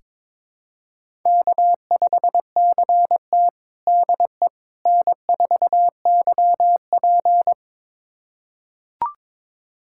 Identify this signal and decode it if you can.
Morse code, heard as a keyed tone, about 22 wpm, 700 Hz. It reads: K5CT DE N4YP
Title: USA Callsign DE Callsign - 4 Characters - 22wpm